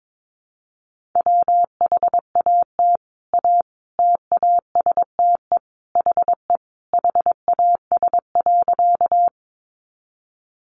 Morse code, heard as a keyed tone, about 22 wpm, 700 Hz. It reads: WHAT A TASTE HE HAS.
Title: Sentences from Top 1100 Words 22wpm